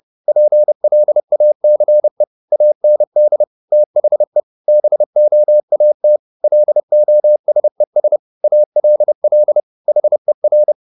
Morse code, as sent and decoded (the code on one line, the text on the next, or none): .--. .-.. .- -.-. / .- -. -.. / - .... / -... --- .- - / .-.. --- ... ... / .- .-.. .-.. / .... .-.
PLACE AND THE BOAT LOSES ALL HER